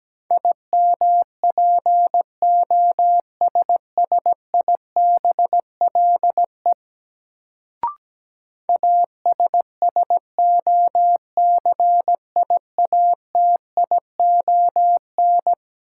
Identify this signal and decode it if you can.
Morse code, heard as a keyed tone, about 17 wpm, 700 Hz. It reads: IMPOSSIBLE ASSOCIATION
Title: Sentences from Top 1100 Words - New Word Drill Alt - 17wpm